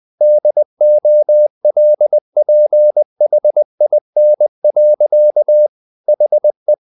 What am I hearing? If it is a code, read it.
Morse code, heard as a keyed tone, about 20 wpm, 600 Hz. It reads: DOLPHIN. HE
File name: the old man and the sea part 2 20wpm